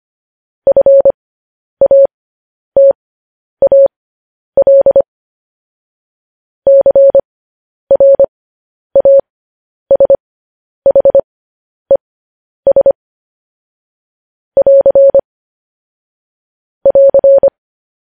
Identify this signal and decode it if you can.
Morse code, heard as a keyed tone, about 25 wpm, spaced as for 10 wpm, 550 Hz. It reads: FATAL CRASHES <AR> <AR>
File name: podcast / News headlines-1x-25x10